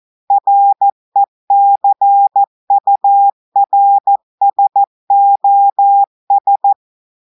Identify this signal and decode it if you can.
Morse code, heard as a keyed tone, about 14 wpm, 800 Hz. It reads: RECURSOS
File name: audios / portugais / portugais texte en clair 2 vitesse 14 mots minutes